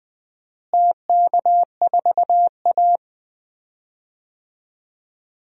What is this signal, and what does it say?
Morse code, heard as a keyed tone, about 21 wpm, 700 Hz. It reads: TK4A
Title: DXCC-100 Call Signs - 4 Characters - Repeated 2x - 20wpm